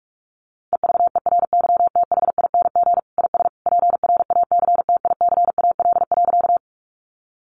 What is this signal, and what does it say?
Morse code, heard as a keyed tone, about 44 wpm, 700 Hz. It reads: EVERYTHING IS PRACTICAL.
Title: Sentences from Top 1400 Words 45wpm